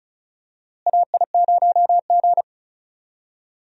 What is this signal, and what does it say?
Morse code, heard as a keyed tone, about 36 wpm, 700 Hz. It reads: AI0G
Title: USA Call Signs - 4 Characters - Repeated 3x - 35wpm